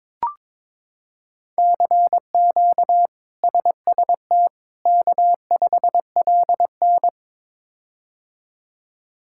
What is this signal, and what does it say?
Morse code, heard as a keyed tone, about 22 wpm, 700 Hz. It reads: CQ SST K5LN
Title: K1USN SST QSO Practice - 22wpm